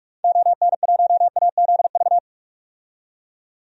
Morse code, heard as a keyed tone, about 46 wpm, 700 Hz. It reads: ON1AZV
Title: DXCC-100 Call Signs - 6 Characters - 45wpm